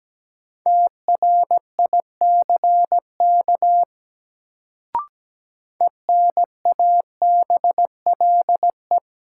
Morse code, heard as a keyed tone, about 17 wpm, 700 Hz. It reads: TRICK ENABLE
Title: Sentences from Top 1500 Words - New Word Drill - 17wpm